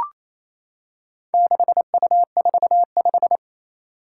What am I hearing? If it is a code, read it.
Morse code, heard as a keyed tone, about 27 wpm, 700 Hz. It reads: 6U45
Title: Four Challenging Letter-Number Combinations 28wpm